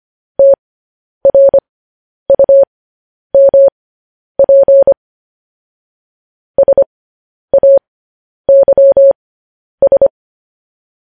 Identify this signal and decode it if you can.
Morse code, heard as a keyed tone, about 25 wpm, spaced as for 10 wpm, 550 Hz. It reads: TRUMP SAYS
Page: news headlines in morse code at 10 wpm